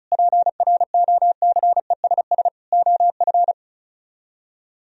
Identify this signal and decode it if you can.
Morse code, heard as a keyed tone, about 45 wpm, 700 Hz. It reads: PROCESS OF